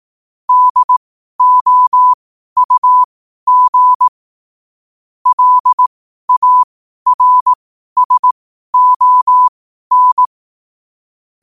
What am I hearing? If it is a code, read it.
Morse code, heard as a keyed tone, about 18 wpm, spaced as for 12 wpm, 1000 Hz. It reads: DOUG LARSON
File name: qotdfiles / 12wpm